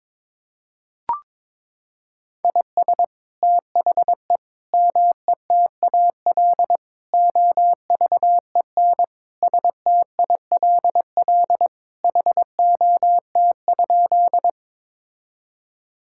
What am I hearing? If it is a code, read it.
Morse code, heard as a keyed tone, about 21 wpm, 700 Hz. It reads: IS THE METAL OVEN STILL HOT?
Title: Sentences from Top 1000 Words - Review Part 6 - 22wpm